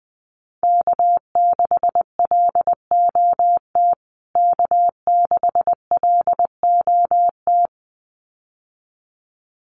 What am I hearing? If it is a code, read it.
Morse code, heard as a keyed tone, about 20 wpm, 700 Hz. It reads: K6LOT K6LOT
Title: USA Call Signs Repeated 2x 20wpm